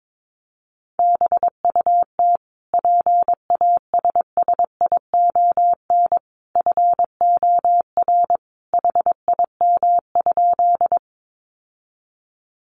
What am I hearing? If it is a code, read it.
Morse code, heard as a keyed tone, about 22 wpm, 700 Hz. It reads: BUT PASSION FOR HIM?
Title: Sentences from Top 1300 Words 22wpm